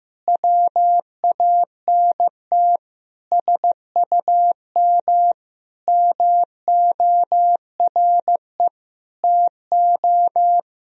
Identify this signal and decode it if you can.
Morse code, heard as a keyed tone, about 15 wpm, 700 Hz. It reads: WANT SUM MORE TO
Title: Sentences from Top 300 Words and Abbreviations 15wpm